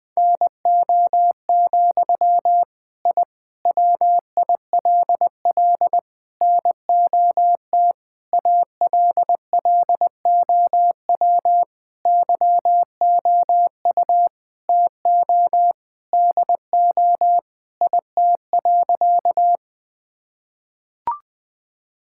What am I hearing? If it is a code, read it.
Morse code, heard as a keyed tone, about 20 wpm, 700 Hz. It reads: NO, I WILL NOT ALLOW YOU TO DO IT.
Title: Sentences from Top 1000 Words - Review Part 5 - 20wpm